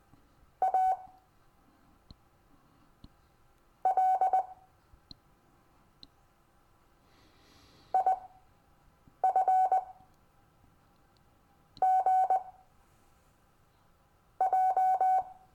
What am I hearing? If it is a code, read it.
Morse code, heard as a keyed tone, about 20 wpm, spaced as for 5 wpm, 750 Hz. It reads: A L IFGJ
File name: cw / Lesson15 five forwardslash